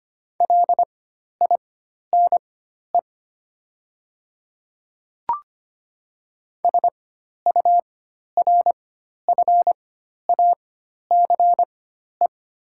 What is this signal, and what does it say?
Morse code, heard as a keyed tone, about 25 wpm, spaced as for 12 wpm, 700 Hz. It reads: LINE SURFACE
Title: 4x Character Spacing - Top 500 Words - 25wpm